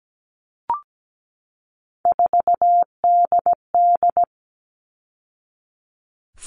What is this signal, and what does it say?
Morse code, heard as a keyed tone, about 17 wpm, 700 Hz. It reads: 4DD
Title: All Cut Numbers - 3-Digit 17wpm